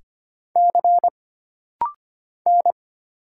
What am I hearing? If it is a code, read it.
Morse code, heard as a keyed tone, about 25 wpm, 700 Hz. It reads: C N
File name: Single Character (Letter:Number:Punctuation:Prosign) - Rapid-Fire - 25wpm